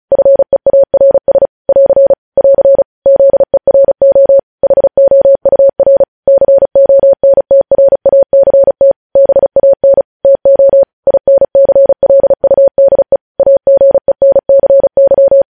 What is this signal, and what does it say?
Morse code, heard as a keyed tone, about 36 wpm, 550 Hz. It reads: FEARS <AR> <AR> ZERO HOUR CONTRACT BAN TO INCLUDE AGENCY
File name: podcast / News headlines-1x-35x35